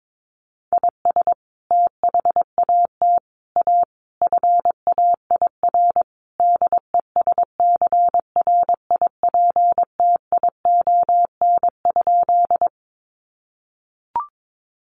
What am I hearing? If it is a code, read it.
Morse code, heard as a keyed tone, about 22 wpm, 700 Hz. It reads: IS THAT A FAIR DESCRIPTION?
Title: Sentences from Top 1300 Words 22wpm